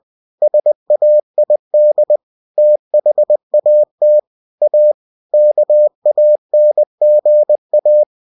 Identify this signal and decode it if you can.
Morse code, heard as a keyed tone, about 20 wpm, 600 Hz. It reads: SAID THAT A KANGA
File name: cw-books / winnie-the-pooh / winnie-20wpm-page-0062